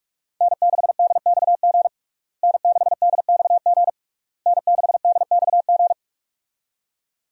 Morse code, heard as a keyed tone, about 46 wpm, 700 Hz. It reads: N6DXG N6DXG N6DXG
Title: USA Call Signs - 5 Characters - Repeated 3x - 45wpm